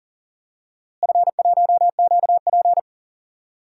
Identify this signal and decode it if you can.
Morse code, heard as a keyed tone, about 40 wpm, 700 Hz. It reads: F1QP
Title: DXCC-100 Call Signs - 4 Characters - Repeated 2x - 40wpm